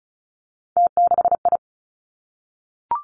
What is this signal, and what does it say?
Morse code, heard as a keyed tone, about 36 wpm, 700 Hz. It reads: T6I
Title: Three Challenging Letter-Number Combinations 35wpm